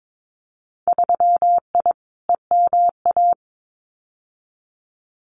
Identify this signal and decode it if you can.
Morse code, heard as a keyed tone, about 22 wpm, 700 Hz. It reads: 3I EMA
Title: Winter Field Day Exchanges Encore 22wpm